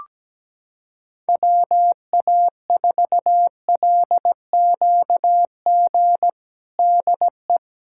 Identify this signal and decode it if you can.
Morse code, heard as a keyed tone, about 17 wpm, 700 Hz. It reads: WA4LQG DE
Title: USA Callsign DE Callsign - 6 Characters - 17wpm